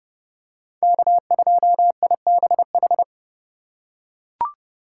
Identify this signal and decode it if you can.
Morse code, heard as a keyed tone, about 27 wpm, 700 Hz. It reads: K2IBH